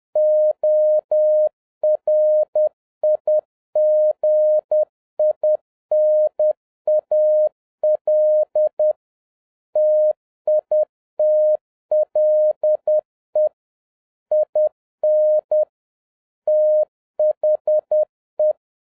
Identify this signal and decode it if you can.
Morse code, heard as a keyed tone, about 10 wpm, 610 Hz. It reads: ORIGINAL TITLE IN THE